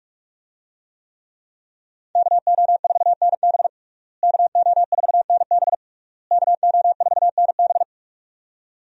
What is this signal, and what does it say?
Morse code, heard as a keyed tone, about 41 wpm, 700 Hz. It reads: KO4NB KO4NB KO4NB
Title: USA Call Signs - 5 Characters - Repeated 3x - 45wpm